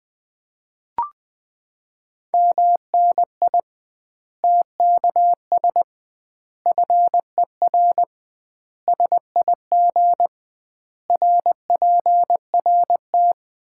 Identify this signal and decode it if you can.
Morse code, heard as a keyed tone, about 20 wpm, 700 Hz. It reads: MNI TKS FER SIG RPRT
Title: Common QSO Phrases - 2x Word-Spacing - 20wpm